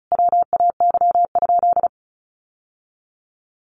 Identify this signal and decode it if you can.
Morse code, heard as a keyed tone, about 34 wpm, 700 Hz. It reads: WAY?
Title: Sentences from Top 1100 Words 35wpm